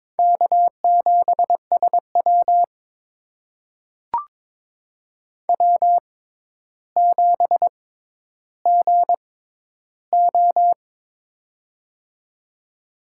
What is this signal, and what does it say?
Morse code, heard as a keyed tone, about 22 wpm, spaced as for 11 wpm, 700 Hz. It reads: K7SW W7GO
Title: USA Call Signs - 6x Character Spacing - 22wpm